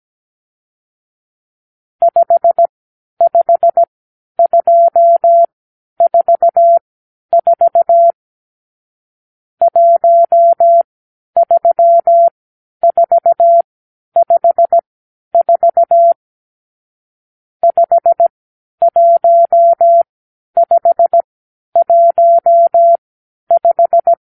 Morse code, heard as a keyed tone, about 17 wpm, spaced as for 10 wpm, 680 Hz. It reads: 55244 13454 51515
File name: arrl / Review (1-5)